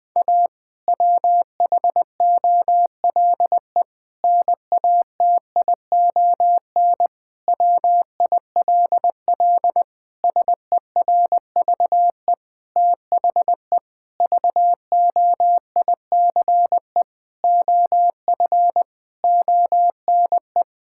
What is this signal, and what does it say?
Morse code, heard as a keyed tone, about 20 wpm, 700 Hz. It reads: A WHOLE NATION WILL SERVE THE VOICE OF ONE
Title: Sentences from Top 1000 Words - Review Part 6 - 20wpm